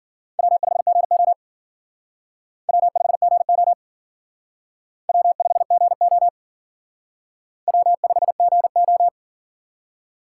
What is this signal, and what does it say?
Morse code, heard as a keyed tone, about 50 wpm, 700 Hz. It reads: W5GO W5GO W5GO W5GO
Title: Speed Racing - USA Call Signs 40wpm